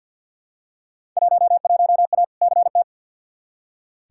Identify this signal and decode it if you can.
Morse code, heard as a keyed tone, about 47 wpm, 700 Hz. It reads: 11A CT
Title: Field Day Exchanges Encore - 50wpm